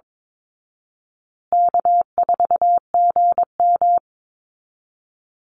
Morse code, heard as a keyed tone, about 22 wpm, 700 Hz. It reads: K4GM
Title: Easy Call Signs Repeated 3x - 22wpm